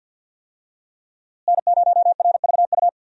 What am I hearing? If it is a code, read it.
Morse code, heard as a keyed tone, about 52 wpm, 700 Hz. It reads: N0RVU
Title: USA Call Signs - 5 Characters - Repeated 3x - 50wpm